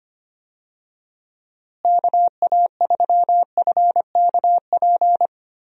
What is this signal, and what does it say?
Morse code, heard as a keyed tone, about 25 wpm, 700 Hz. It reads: KA3FKP
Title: USA Call Signs - 6 Characters - Repeated 3x - 25wpm